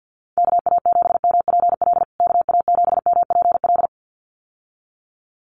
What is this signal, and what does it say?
Morse code, heard as a keyed tone, about 50 wpm, 700 Hz. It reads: KA7MPL KA7MPL
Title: USA Call Signs Repeated 2x 50wpm